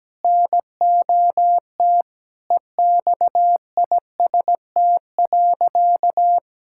Morse code, heard as a keyed tone, about 17 wpm, 700 Hz. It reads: NOT EXIST.
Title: Sentences from Top 1000 Words - Review Part 5 - 17wpm